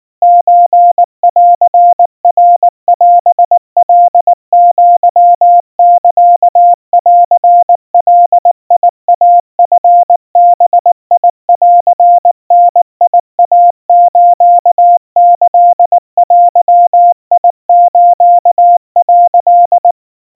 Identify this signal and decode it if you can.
Morse code (Japanese Wabun code), heard as a keyed tone, about 19 wpm, 700 Hz. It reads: ソンナオカアサンガイチバンダイスキデス」
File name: kawa / Morse sample / kamyu 70 cpm 700 Hz Crispy